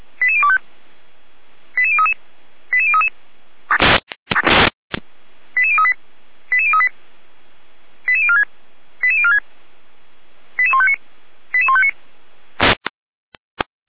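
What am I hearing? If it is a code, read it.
Five-tone selective call (ZVEI), heard as quick runs of five tones, tones of about 70 ms. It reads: 88025 88033 88033 88038 88038 88046 88046 88159 88159
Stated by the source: ZVEI five-tone call received by SDR.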